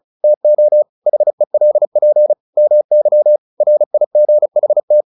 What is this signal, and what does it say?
Morse code, heard as a keyed tone, about 34 wpm, 600 Hz. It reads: TO HELP MY RIGHT
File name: the old man and the sea / the old man and the sea 1 35wpm